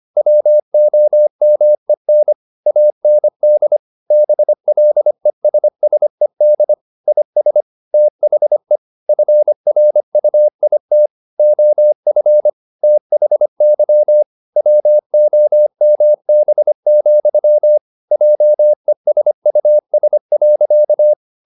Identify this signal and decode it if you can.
Morse code, heard as a keyed tone, about 25 wpm, 600 Hz. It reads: WOMEN AND BLESSED IS THE FRUIT OF THY WOMB, JESUS.
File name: the old man and the sea part 1 25wpm